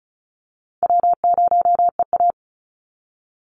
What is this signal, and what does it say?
Morse code, heard as a keyed tone, about 34 wpm, 700 Hz. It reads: W0EA
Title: Easy Call Signs Repeated 2x - 35wpm